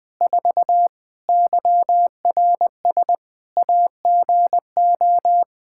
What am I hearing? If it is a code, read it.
Morse code, heard as a keyed tone, about 20 wpm, 700 Hz. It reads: 4 YRS AGO
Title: Practice QSO - Exchange 3 - More Info - 20wpm